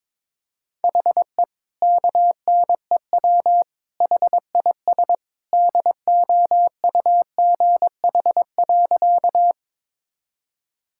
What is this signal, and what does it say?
Morse code, heard as a keyed tone, about 22 wpm, 700 Hz. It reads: HE KNEW HIS DOUGH.
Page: morse-code-ninja-practice-22wpm